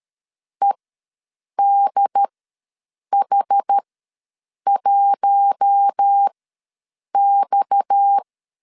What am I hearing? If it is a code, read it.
Morse code, heard as a keyed tone, about 13 wpm, spaced as for 7 wpm, 800 Hz. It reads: EDH1X
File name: LEZIONI / 6 PKX0 / 6 PKX ws 12wpm 3106